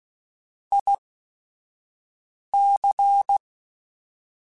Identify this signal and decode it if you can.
Morse code, heard as a keyed tone, about 16 wpm, spaced as for 5 wpm, 800 Hz. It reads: IC